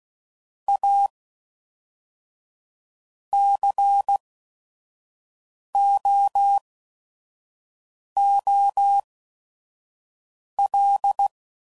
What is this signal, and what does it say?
Morse code, heard as a keyed tone, about 16 wpm, spaced as for 5 wpm, 800 Hz. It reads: A COOL
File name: CW Code Groups @5WPM / Weather Sets / Weather 4 14